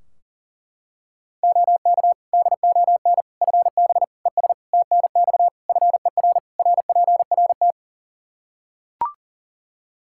Morse code, heard as a keyed tone, about 37 wpm, 700 Hz. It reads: OK DON FB ES TNX FER RPRT
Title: Practice QSO - Exchange 2 - Info - 40wpm